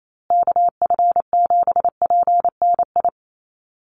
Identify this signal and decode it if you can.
Morse code, heard as a keyed tone, about 28 wpm, 700 Hz. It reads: KF7PNI
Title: USA Call Signs Repeated 3x 28wpm